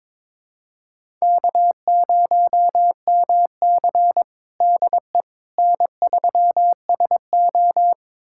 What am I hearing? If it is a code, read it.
Morse code, heard as a keyed tone, about 22 wpm, 700 Hz. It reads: K0MC DE N3SO
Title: USA Callsign DE Callsign - 4 Characters - 22wpm